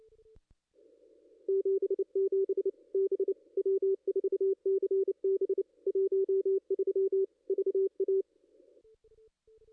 Morse code, heard as a keyed tone, about 31 wpm, 440 Hz. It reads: 77 B W4CB 13 VA
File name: contests / ss-cw-k5tr-op-n5rz -2200